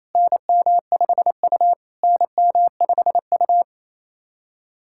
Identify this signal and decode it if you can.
Morse code, heard as a keyed tone, about 27 wpm, 690 Hz. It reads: NM5U NM5U
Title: SOTA QSO Practice 28wpm